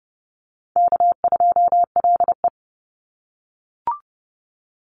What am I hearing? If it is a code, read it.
Morse code, heard as a keyed tone, about 30 wpm, 700 Hz. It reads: K2LE
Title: Easy Call Signs - 30wpm